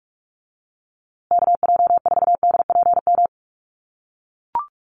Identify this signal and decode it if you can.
Morse code, heard as a keyed tone, about 44 wpm, 700 Hz. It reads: KJ4DPM
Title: USA Call Signs Repeated 2x 45wpm